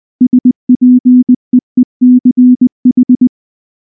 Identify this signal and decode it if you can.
Morse code, heard as a keyed tone, about 20 wpm, 250 Hz. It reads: SPEECH